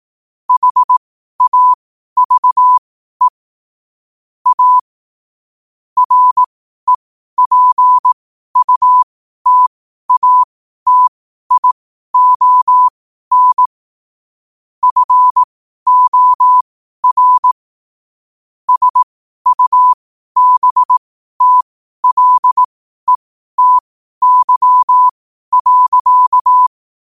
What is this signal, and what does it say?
Morse code, heard as a keyed tone, about 18 wpm, spaced as for 12 wpm, 1000 Hz. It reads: HAVE A REPUTATION FOR SUBTLETY.
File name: qotdfiles / 12wpm